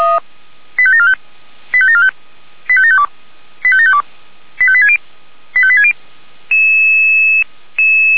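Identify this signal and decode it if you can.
Five-tone selective call (ZVEI), heard as quick runs of five tones, tones of about 70 ms. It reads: 85745 85745 85752 85752 85860 85860
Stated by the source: ZVEI five-tone call 85727 received by SDR.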